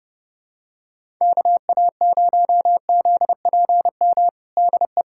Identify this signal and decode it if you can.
Morse code, heard as a keyed tone, about 30 wpm, 700 Hz. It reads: KA0ZPM DE
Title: USA Callsign DE Callsign - 6 Characters - 30wpm